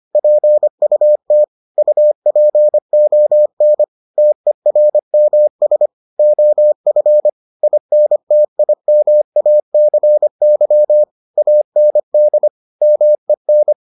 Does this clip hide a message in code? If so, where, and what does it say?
Yes: Morse code from 0.2 to 13.7 s, heard as a keyed tone, about 25 wpm, 600 Hz. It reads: PUT UPON TERMS OF INTIMACY AND MEN